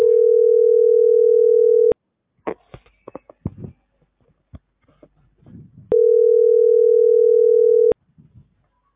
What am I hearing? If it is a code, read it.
Ringback tone (North America), 440 + 480 Hz.